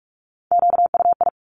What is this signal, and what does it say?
Morse code, heard as a keyed tone, about 46 wpm, 700 Hz. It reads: QUI
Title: Top Three Letter Combinations in Words 45wpm